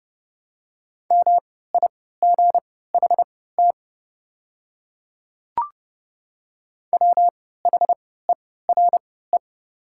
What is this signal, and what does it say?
Morse code, heard as a keyed tone, about 32 wpm, spaced as for 17 wpm, 700 Hz. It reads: MIGHT WHERE
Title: 3x Character Spacing - Top 300 Words - 30wpm